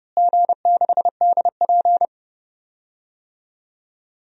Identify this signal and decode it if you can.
Morse code, heard as a keyed tone, about 30 wpm, 700 Hz. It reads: G6DP
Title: DXCC-100 Call Signs - 4 Characters - Repeated 3x - 30wpm